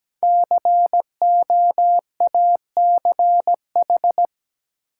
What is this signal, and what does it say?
Morse code, heard as a keyed tone, about 17 wpm, 700 Hz. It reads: COACH
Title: Five Letter Words - 17wpm